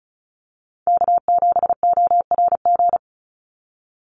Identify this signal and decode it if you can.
Morse code, heard as a keyed tone, about 34 wpm, 700 Hz. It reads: K7ORG
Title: NA SOTA Chaser Callsigns 35wpm